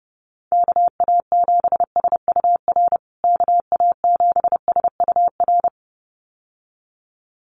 Morse code, heard as a keyed tone, about 30 wpm, 700 Hz. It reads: KA7SUR KA7SUR
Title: USA Call Signs Repeated 2x 30wpm